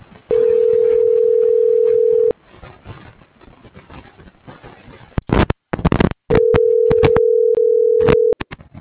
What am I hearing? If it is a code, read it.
Ringback tone (North America), 440 + 480 Hz.